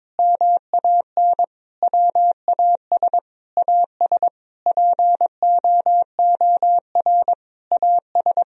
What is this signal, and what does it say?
Morse code, heard as a keyed tone, about 22 wpm, 700 Hz. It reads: MAN WAS AS POOR AS
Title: Sentences from Top 1000 Words - Review Part 2 - 22wpm